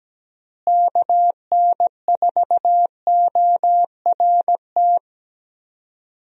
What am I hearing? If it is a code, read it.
Morse code, heard as a keyed tone, about 17 wpm, 700 Hz. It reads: KN4ORT
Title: USA Call Signs - 6 Characters - Repeated 2x - 17wpm